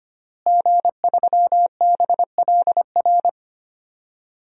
Morse code, heard as a keyed tone, about 25 wpm, 700 Hz. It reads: G3BLR
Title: DXCC-100 Call Signs - 5 Characters - Repeated 3x - 25wpm